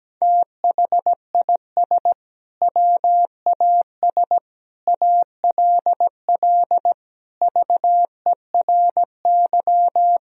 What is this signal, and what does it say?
Morse code, heard as a keyed tone, about 17 wpm, 700 Hz. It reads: THIS WAS ALL VERY
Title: Sentences from Top 1000 Words - Review Part 3 - 17wpm